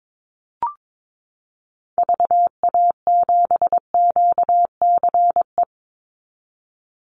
Morse code, heard as a keyed tone, about 22 wpm, 700 Hz. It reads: VA7QCE
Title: Call Signs Repeated 3x - 22wpm